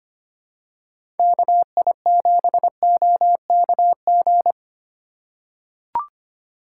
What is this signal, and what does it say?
Morse code, heard as a keyed tone, about 25 wpm, 700 Hz. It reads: KI7OKG
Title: USA Call Signs - 6 Characters - 3x Character Spacing - 25wpm